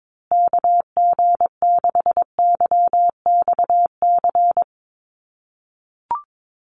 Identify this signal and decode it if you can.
Morse code, heard as a keyed tone, about 22 wpm, 700 Hz. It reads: KG6YXC
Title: USA Call Signs Repeated 2x 22wpm